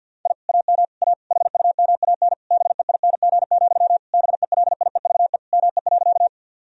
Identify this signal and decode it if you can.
Morse code, heard as a keyed tone, about 50 wpm, 700 Hz. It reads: I AM A HUMAN BEING, BELIEVE ME.
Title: Sentences from Top 1000 Words - Review Part 3 - 50wpm